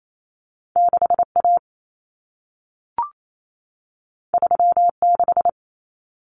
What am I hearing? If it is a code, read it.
Morse code, heard as a keyed tone, about 28 wpm, 700 Hz. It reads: 6A 36
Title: Common Cut Numbers - 2-Digit 28wpm